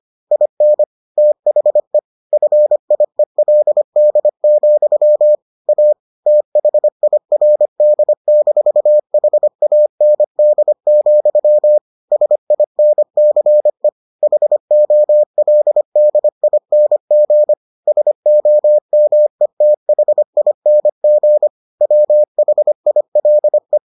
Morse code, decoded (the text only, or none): IN THE FIELD, A THIRD-HAND, SINCE HOLDING SOMETHING WHILE